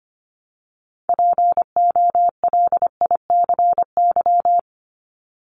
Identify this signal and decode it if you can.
Morse code, heard as a keyed tone, about 25 wpm, 700 Hz. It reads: POLICY
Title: Top 501-750 Words 25wpm